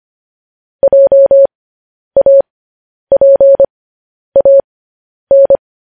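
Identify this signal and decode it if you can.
Morse code, heard as a keyed tone, about 25 wpm, spaced as for 10 wpm, 550 Hz. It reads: JAPAN
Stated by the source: Morse code news headlines practice transmission.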